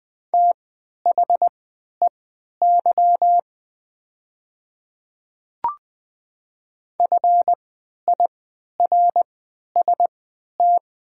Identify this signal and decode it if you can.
Morse code, heard as a keyed tone, about 20 wpm, spaced as for 11 wpm, 700 Hz. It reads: THEY FIRST
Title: 3x Character Spacing - Top 500 Words - 20wpm